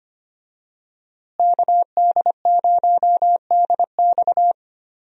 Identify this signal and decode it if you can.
Morse code, heard as a keyed tone, about 25 wpm, 700 Hz. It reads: KD0DX
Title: USA Call Signs - 5 Characters - Repeated 2x - 25wpm